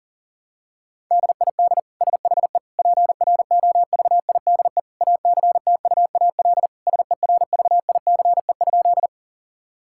Morse code, heard as a keyed tone, about 40 wpm, 700 Hz. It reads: DID SHE PROVIDE ACTUAL SERVICE?
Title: Sentences from Top 1000 Words - Review Part 5 - 40wpm